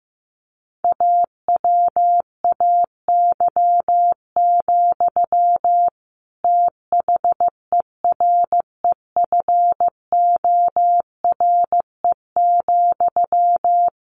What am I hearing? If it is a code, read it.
Morse code, heard as a keyed tone, about 15 wpm, 700 Hz. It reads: AWAY, THEREFORE,
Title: Sentences from Top 1500 Words 15wpm